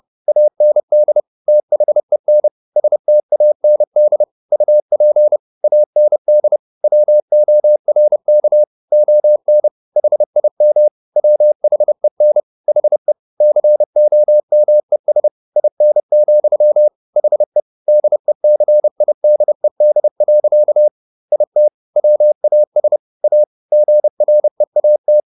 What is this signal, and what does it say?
Morse code, heard as a keyed tone, about 30 wpm, 600 Hz. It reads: AND THEN STAND UP AND WORK ON HIM WHEN HE COMES IN, HE DECIDED. IT WAS A GREAT